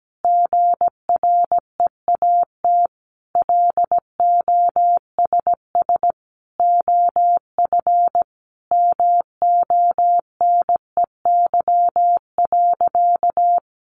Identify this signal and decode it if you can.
Morse code, heard as a keyed tone, about 17 wpm, 700 Hz. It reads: GREAT LOSS OF MONEY.
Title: Sentences from Top 1400 Words 17wpm